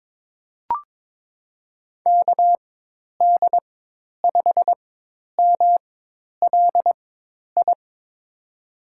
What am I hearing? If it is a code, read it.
Morse code, heard as a keyed tone, about 22 wpm, spaced as for 10 wpm, 700 Hz. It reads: KD5MLI
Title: USA Call Signs - 6 Characters - 4x Character Spacing - 22wpm